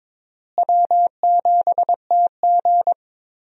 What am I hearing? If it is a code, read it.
Morse code, heard as a keyed tone, about 22 wpm, 700 Hz. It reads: W7TG
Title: USA Call Signs - 4 Characters - Repeated 2x - 22wpm